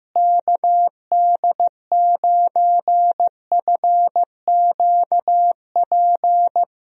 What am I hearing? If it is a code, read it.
Morse code, heard as a keyed tone, about 15 wpm, 700 Hz. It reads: KD9FQP
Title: USA Call Signs - 6 Characters - Repeated 2x - 15wpm